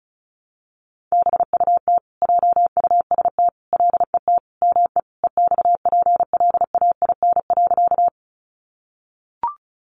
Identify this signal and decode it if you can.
Morse code, heard as a keyed tone, about 34 wpm, 700 Hz. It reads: BUT JUST LET ME EXPLAIN.
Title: Sentences from Top 1100 Words 35wpm